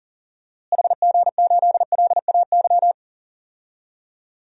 Morse code, heard as a keyed tone, about 38 wpm, 700 Hz. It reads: HG8LAY
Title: DXCC-100 Call Signs - 6 Characters - Repeated 2x - 40wpm